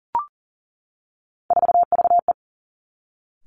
Morse code, heard as a keyed tone, about 42 wpm, 690 Hz. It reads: 4VE
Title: Three Challenging Letter-Number Combinations 40wpm